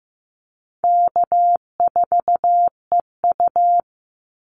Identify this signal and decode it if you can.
Morse code, heard as a keyed tone, about 14 wpm, 700 Hz. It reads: K4EU
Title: Easy Call Signs Repeated 3x - 15wpm